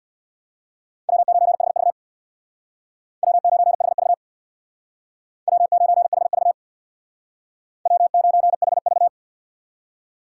Morse code, heard as a keyed tone, about 62 wpm, 700 Hz. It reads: W9HV W9HV W9HV W9HV
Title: Speed Racing - USA 1x2 Call Signs 50wpm